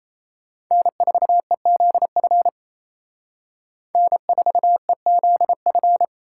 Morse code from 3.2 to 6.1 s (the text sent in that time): N4EZF